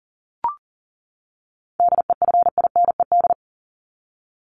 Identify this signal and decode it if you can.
Morse code, heard as a keyed tone, about 40 wpm, 700 Hz. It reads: DEFINED
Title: ED Suffix - Top 100 Words with Base Words 40wpm